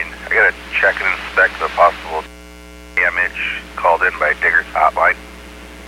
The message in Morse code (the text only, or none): none